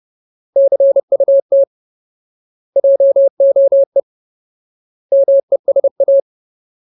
Morse code, heard as a keyed tone, about 30 wpm, 550 Hz. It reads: CUT JOE MESA
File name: Ultimate Word List / 30WPM Ultimate